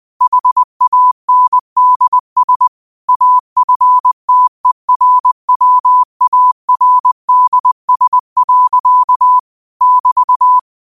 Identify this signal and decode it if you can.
Morse code, heard as a keyed tone, about 20 wpm, 1000 Hz. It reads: HANDS AFTERWARDS. <BT>
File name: 2025-06-25-20wpm